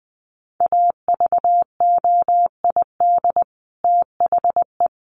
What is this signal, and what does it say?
Morse code, heard as a keyed tone, about 20 wpm, 700 Hz. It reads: AVOID THE
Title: Sentences from Top 1400 Words 20wpm